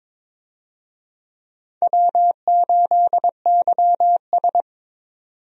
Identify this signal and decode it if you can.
Morse code, heard as a keyed tone, about 22 wpm, 700 Hz. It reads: W8YS
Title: USA Call Signs - 4 Characters - 22wpm